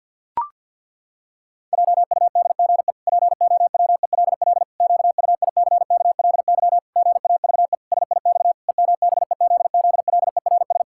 Morse code, heard as a keyed tone, about 48 wpm, 700 Hz. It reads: JUDGE POWELL QUICKLY GAVE SIX EMBEZZLERS
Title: Pangram Challenge v2 - 50wpm